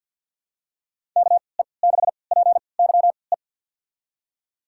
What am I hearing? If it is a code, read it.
Morse code, heard as a keyed tone, about 52 wpm, spaced as for 29 wpm, 700 Hz. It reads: KE6PYE
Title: USA Call Signs - 6 Characters - 3x Character Spacing - 50wpm